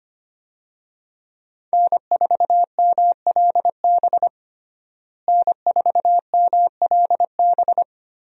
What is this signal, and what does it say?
Morse code, heard as a keyed tone, about 25 wpm, 700 Hz. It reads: N4MLB N4MLB
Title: USA Call Signs - 5 Characters - Repeated 3x - 25wpm